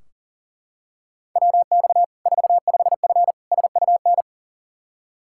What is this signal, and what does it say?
Morse code, heard as a keyed tone, about 40 wpm, 700 Hz. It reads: WX 45F SUN
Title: K1USN SST QSO Practice 40wpm